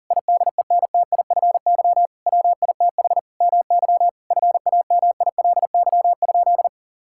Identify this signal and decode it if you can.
Morse code, heard as a keyed tone, about 40 wpm, 700 Hz. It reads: IDENTIFY WITH MY FAMILY?
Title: Sentences from Top 1000 Words - Review Part 6 - 40wpm